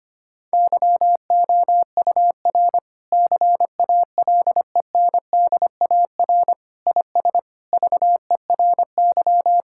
Morse code, heard as a keyed tone, about 25 wpm, 700 Hz. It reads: YOUR CALENDAR IS VERY
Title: Sentences from Top 1600 Words 25wpm